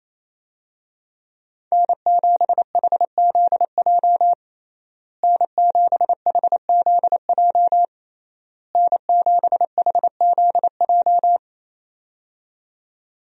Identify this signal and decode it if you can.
Morse code, heard as a keyed tone, about 28 wpm, 700 Hz. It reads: N7HZJ N7HZJ N7HZJ
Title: USA Call Signs - 5 Characters - Repeated 3x - 28wpm